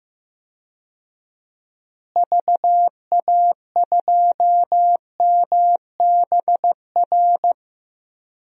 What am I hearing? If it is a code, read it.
Morse code, heard as a keyed tone, about 16 wpm, 700 Hz. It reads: VA2MBR
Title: DXCC-100 Call Signs - 6 Characters - Repeated 3x - 15wpm